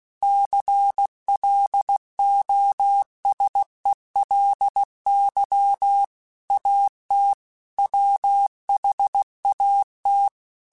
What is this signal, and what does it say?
Morse code, heard as a keyed tone, about 16 wpm, 800 Hz. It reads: CLOSELY AT WHAT
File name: Technical Subjects / Technical Subjects-mp3s / OR Gate @15